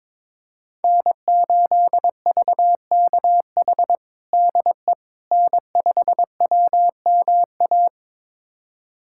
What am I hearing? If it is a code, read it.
Morse code, heard as a keyed tone, about 22 wpm, 700 Hz. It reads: N8VKH DE N5WMA
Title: USA Callsign DE Callsign - 5 Characters - 22wpm